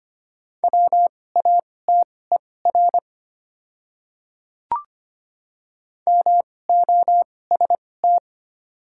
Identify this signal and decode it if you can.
Morse code, heard as a keyed tone, about 25 wpm, spaced as for 18 wpm, 690 Hz. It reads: WATER MOST
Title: 2x Character Spacing - Top 100 Words - 25wpm